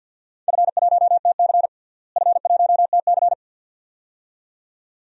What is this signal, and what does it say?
Morse code, heard as a keyed tone, about 52 wpm, 700 Hz. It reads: F1TC F1TC
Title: DXCC-100 Call Signs - 4 Characters - Repeated 2x - 50wpm